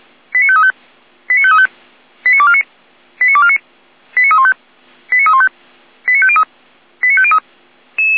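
Five-tone selective call (ZVEI), heard as quick runs of five tones, tones of about 70 ms. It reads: 89535 89535 89249 89249 89315 89315 89503 89503
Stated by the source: ZVEI five-tone selective call received on SDR.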